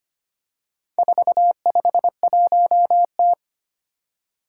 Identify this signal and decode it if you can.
Morse code, heard as a keyed tone, about 25 wpm, 700 Hz. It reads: 451T
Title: Four Challenging Letter-Number Combinations 25wpm